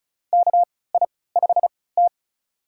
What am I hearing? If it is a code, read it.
Morse code, heard as a keyed tone, about 34 wpm, spaced as for 20 wpm, 700 Hz. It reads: KI5T
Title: USA Call Signs - 4 Characters - 3x Character Spacing - 35wpm